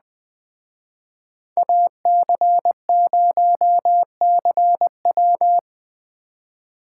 Morse code, heard as a keyed tone, about 21 wpm, 700 Hz. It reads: AC0CW